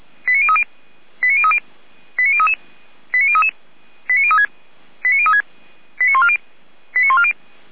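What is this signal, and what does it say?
Five-tone selective call (ZVEI), heard as quick runs of five tones, tones of about 70 ms. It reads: 89030 89030 89033 89033 89036 89036 89140 89140
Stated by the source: SDR-received ZVEI five-tone selective call.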